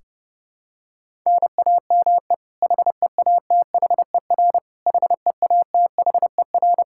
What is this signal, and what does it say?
Morse code, heard as a keyed tone, about 30 wpm, 700 Hz. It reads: NAME HEATHER HEATHER
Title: K1USN SST QSO Practice 30wpm